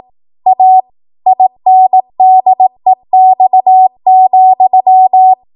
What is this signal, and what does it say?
Morse code, heard as a keyed tone, about 18 wpm, 750 Hz. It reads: A INDEX,
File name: Archive / 18 WPM / 18WPM